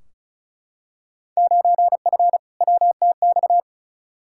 Morse code, heard as a keyed tone, about 34 wpm, 700 Hz. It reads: 9F WTX
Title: Field Day Exchanges Encore - 35wpm